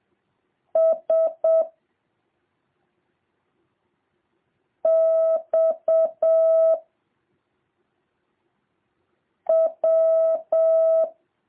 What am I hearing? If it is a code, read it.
Morse code, heard as a keyed tone, about 7 wpm, 650 Hz. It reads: S X W